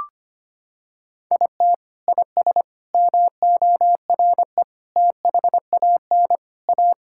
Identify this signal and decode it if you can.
Morse code, heard as a keyed tone, about 25 wpm, 700 Hz. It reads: IT IS MORE THAN A
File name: practice / 25wpm / Sentences with 3-Word Phrases - 25wpm